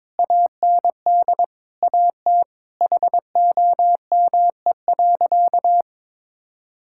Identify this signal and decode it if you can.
Morse code, heard as a keyed tone, about 22 wpm, 700 Hz. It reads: AND AT HOME.
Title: Sentences from Top 1000 Words - Review Part 6 - 22wpm